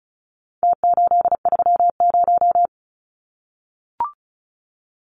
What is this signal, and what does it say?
Morse code, heard as a keyed tone, about 34 wpm, 700 Hz. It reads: T830
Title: Numbers - Contest Style - 4-Digit 35wpm